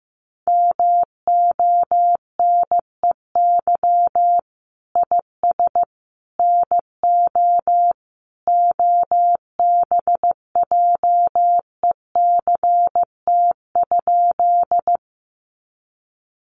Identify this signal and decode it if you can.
Morse code, heard as a keyed tone, about 15 wpm, 700 Hz. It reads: MONEY IS NO OBJECT?
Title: Sentences from Top 500 Words 15wpm